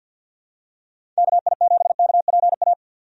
Morse code, heard as a keyed tone, about 50 wpm, 700 Hz. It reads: KI7KPA
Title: USA Call Signs - 6 Characters - 50wpm